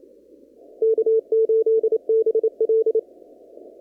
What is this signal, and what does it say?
Morse code, heard as a keyed tone, about 28 wpm, 440 Hz. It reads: K8BL
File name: CW / kt5j wpx cw-op-k5pi